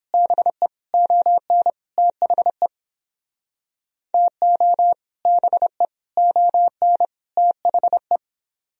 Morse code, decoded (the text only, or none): BE ON THE TO BE ON THE